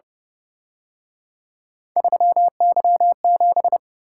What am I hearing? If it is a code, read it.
Morse code, heard as a keyed tone, about 30 wpm, 700 Hz. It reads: 3Y7
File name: Speed-Racing - Three Letter-Number Combinations - 25wpm